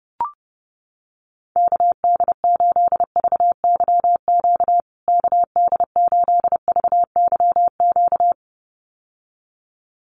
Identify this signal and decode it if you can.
Morse code, heard as a keyed tone, about 30 wpm, 700 Hz. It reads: KD8VYQ KD8VYQ
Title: USA Call Signs Repeated 2x 30wpm